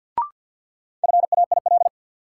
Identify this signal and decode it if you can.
Morse code, heard as a keyed tone, about 50 wpm, 700 Hz. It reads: FAIL
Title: Sentences from Top 1100 Words - New Word Drill - 50wpm